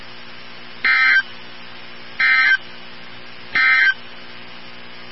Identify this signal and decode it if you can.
Emergency Alert System (SAME) tones, heard as harsh buzzing bursts. It is heard from 0.7 to 4.0 s.
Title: NOAA Weather Radio 162.550